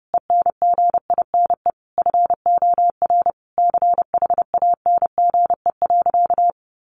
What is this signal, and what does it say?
Morse code, heard as a keyed tone, about 28 wpm, 700 Hz. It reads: ENGINE FOR CHANGE.
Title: Sentences from Top 1500 Words 30wpm